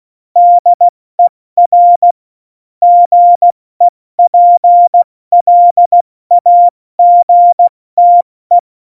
Morse code, heard as a keyed tone, about 16 wpm, 700 Hz. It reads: DER GEPLAGTE
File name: Morsekurs / Witze / Witz 21 Juli 2020, 70 F80